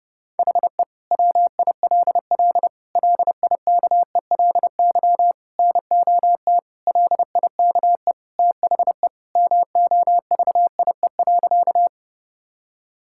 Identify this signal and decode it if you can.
Morse code, heard as a keyed tone, about 32 wpm, 700 Hz. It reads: HE WILL LIKELY NOT LIKE THE MOVIE.
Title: Sentences from Top 1000 Words - Review Part 5 - 30wpm